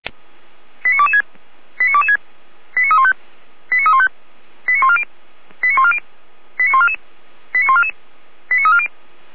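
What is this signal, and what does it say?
Five-tone selective call (ZVEI), heard as quick runs of five tones, tones of about 70 ms. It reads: 89226 89226 89315 89315 89140 89140 89144 89144 89340
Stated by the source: ZVEI five-tone call 89127 received by SDR, archived by feuerwehr-orlamuende.org fire brigade.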